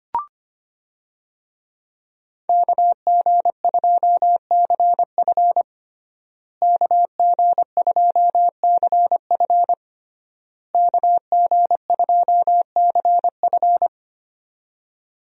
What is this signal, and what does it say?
Morse code, heard as a keyed tone, about 25 wpm, 700 Hz. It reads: KG2CF KG2CF KG2CF
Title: USA Call Signs - 5 Characters - Repeated 3x - 25wpm